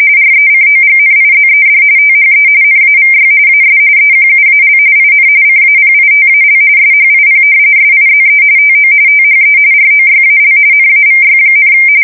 RTTY, about 45.45 baud. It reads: RECOGNIZED THE NAVY AS EQUAL IN CAPITAL SHIP SIZE TO THE ROYAL NAVY,
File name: fsk170 / usnavy1